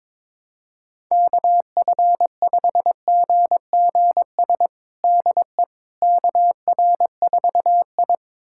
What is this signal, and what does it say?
Morse code, heard as a keyed tone, about 22 wpm, 700 Hz. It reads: KF5GGS DE KR4I
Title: USA Callsign DE Callsign 22wpm